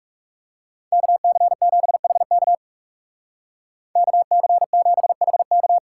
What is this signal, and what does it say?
Morse code, heard as a keyed tone, about 35 wpm, 700 Hz. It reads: KC7HK KC7HK
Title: Speed Racing - USA 2x2 Call Signs 30wpm